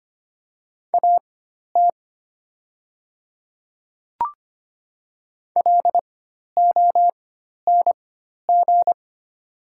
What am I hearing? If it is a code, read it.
Morse code, heard as a keyed tone, about 25 wpm, spaced as for 12 wpm, 700 Hz. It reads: AT LONG
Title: 4x Character Spacing - Top 100 Words - 25wpm